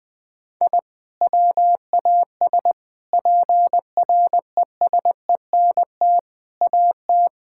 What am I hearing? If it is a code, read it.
Morse code, heard as a keyed tone, about 20 wpm, 700 Hz. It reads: I WAS PRESENT AT
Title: Sentences from Top 1000 Words - Review Part 5 - 20wpm